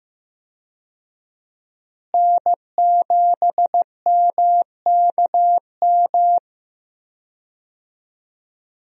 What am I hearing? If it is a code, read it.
Morse code, heard as a keyed tone, about 15 wpm, 700 Hz. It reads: N7MKM
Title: USA Call Signs - 5 Characters - Repeated 2x - 15wpm